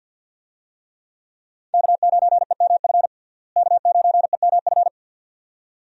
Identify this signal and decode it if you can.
Morse code, heard as a keyed tone, about 51 wpm, 700 Hz. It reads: K9EMF K9EMF
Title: USA Call Signs - 5 Characters - Repeated 2x - 50wpm